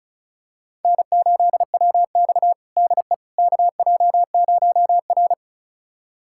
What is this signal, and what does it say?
Morse code, heard as a keyed tone, about 36 wpm, 700 Hz. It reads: N8WX DE KJ0R
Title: USA Callsign DE Callsign - 4 Characters - 35wpm